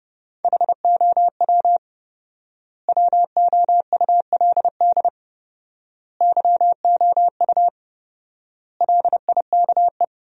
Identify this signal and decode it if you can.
Morse code, heard as a keyed tone, about 30 wpm, 700 Hz. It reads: HOW WOULD YOU LIKE
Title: 4x Word-Spacing - Sentences from Top 100 Words - 30wpm